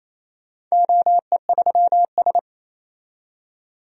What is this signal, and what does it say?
Morse code, heard as a keyed tone, about 28 wpm, 700 Hz. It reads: OE3S
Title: DXCC-100 Call Signs - 4 Characters - 28wpm